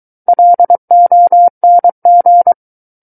Morse code, heard as a keyed tone, about 23 wpm, 700 Hz. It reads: LONG
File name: Book One / 14-Chapter 14 - 23 WPM